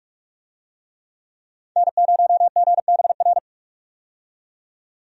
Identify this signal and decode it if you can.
Morse code, heard as a keyed tone, about 48 wpm, 700 Hz. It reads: N0GBR
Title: USA Call Signs - 5 Characters - 45wpm